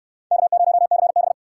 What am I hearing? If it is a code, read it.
Morse code, heard as a keyed tone, about 69 wpm, 700 Hz. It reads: K0CB